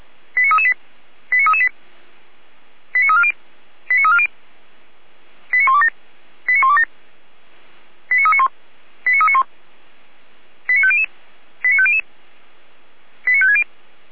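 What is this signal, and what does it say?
Five-tone selective call (ZVEI), heard as quick runs of five tones, tones of about 70 ms. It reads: 89338 89338 89340 89340 89127 89127 89391 89391 89500 89500 89560
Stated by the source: ZVEI five-tone selective call received on SDR.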